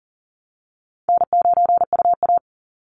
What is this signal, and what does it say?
Morse code, heard as a keyed tone, about 40 wpm, 700 Hz. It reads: N9UA
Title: Easy Call Signs Repeated 3x - 40wpm